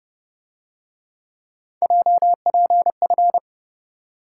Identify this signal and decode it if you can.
Morse code, heard as a keyed tone, about 30 wpm, 700 Hz. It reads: JPF